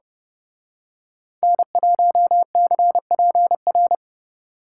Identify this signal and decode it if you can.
Morse code, heard as a keyed tone, about 28 wpm, 700 Hz. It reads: N1CPR
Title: Speed Racing - USA 1x3 Call Signs 30wpm